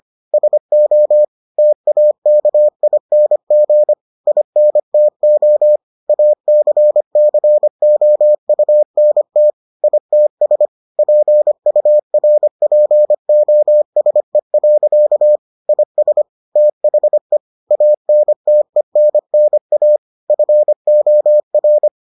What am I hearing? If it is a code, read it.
Morse code, heard as a keyed tone, about 25 wpm, 600 Hz. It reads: SO TAKING INTO ACCOUNT ITS PURPOSE. IS THE ANTENNA FOR